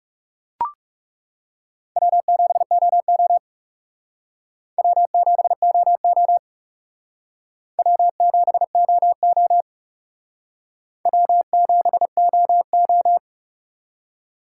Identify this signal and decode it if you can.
Morse code, heard as a keyed tone, about 37 wpm, 700 Hz. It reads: W7OO W7OO W7OO W7OO
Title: Speed Racing - USA 1x2 Call Signs 30wpm